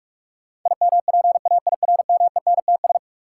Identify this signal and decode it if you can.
Morse code, heard as a keyed tone, about 47 wpm, 700 Hz. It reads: IMPAIRMENTS
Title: Sets of 5 Words Encore - New Word Drill - 45wpm